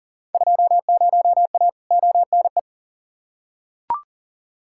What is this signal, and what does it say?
Morse code, heard as a keyed tone, about 40 wpm, 700 Hz. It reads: 20A ONE
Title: Field Day Exchanges Encore - 40wpm